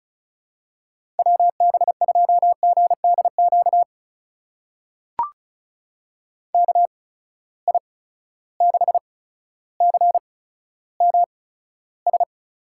Morse code, heard as a keyed tone, about 36 wpm, spaced as for 15 wpm, 700 Hz. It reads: WB2GDQ KI6CMS
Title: USA Call Signs - 6 Characters - 8x Character Spacing - 35wpm